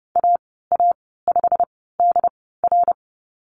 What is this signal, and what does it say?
Morse code, heard as a keyed tone, about 30 wpm, spaced as for 17 wpm, 690 Hz. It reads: AA5DR
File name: Lesson 15g - Number 5 - Call Signs 30wpm